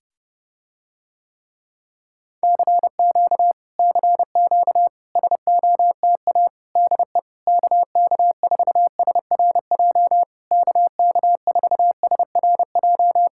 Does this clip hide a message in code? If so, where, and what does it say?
Yes: Morse code from 2.4 to 13.3 s, heard as a keyed tone, about 29 wpm, 700 Hz. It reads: CQ CQ SOTA DE KK4SRJ KK4SRJ